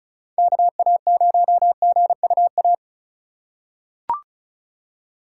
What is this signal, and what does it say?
Morse code, heard as a keyed tone, about 36 wpm, 700 Hz. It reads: KA0GUA
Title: USA Call Signs - 6 Characters - 3x Character Spacing - 35wpm